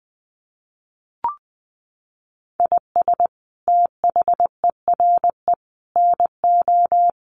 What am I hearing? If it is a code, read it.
Morse code, heard as a keyed tone, about 20 wpm, 700 Hz. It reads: IS THERE NO
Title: Sentences from Top 1500 Words 20wpm